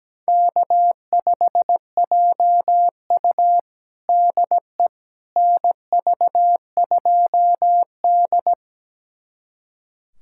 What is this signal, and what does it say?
Morse code, heard as a keyed tone, about 17 wpm, 700 Hz. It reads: K5JU DE NV2D
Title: USA Callsign DE Callsign - 4 Characters - 17wpm